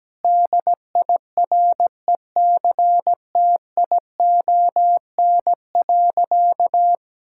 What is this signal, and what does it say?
Morse code, heard as a keyed tone, about 17 wpm, 700 Hz. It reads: DIRECTION.
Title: Sentences from Top 1000 Words - Review Part 5 - 17wpm